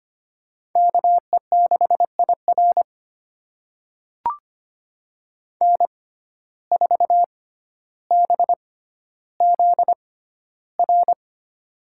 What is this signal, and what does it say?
Morse code, heard as a keyed tone, about 25 wpm, spaced as for 13 wpm, 700 Hz. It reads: KE6IR N4BZR
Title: USA Call Signs - 5 Characters - 6x Character Spacing - 25wpm